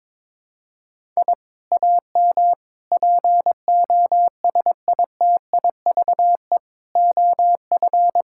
Morse code, heard as a keyed tone, about 22 wpm, 700 Hz. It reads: I AM POSITIVE OF
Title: Sentences from Top 1000 Words - Review Part 5 - 22wpm